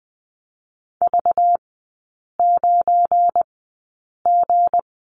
Morse code, heard as a keyed tone, about 20 wpm, 700 Hz. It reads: V 9 G
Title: 198 Lesson 32e - Letter k - Three Characters 20wpm